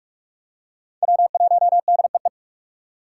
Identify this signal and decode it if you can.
Morse code, heard as a keyed tone, about 47 wpm, 700 Hz. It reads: W1DEE
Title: USA Call Signs - 5 Characters - 6x Character Spacing - 45wpm